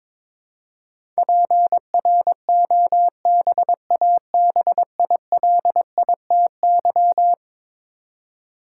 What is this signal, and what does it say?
Morse code, heard as a keyed tone, about 22 wpm, 700 Hz. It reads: PROBABILITY
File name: practice / 22wpm / Eleven Letter Words - 22wpm